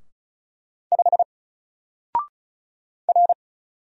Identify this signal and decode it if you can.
Morse code, heard as a keyed tone, about 33 wpm, 700 Hz. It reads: 5 R
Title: Single Character (Letter:Number:Punctuation:Prosign) 35wpm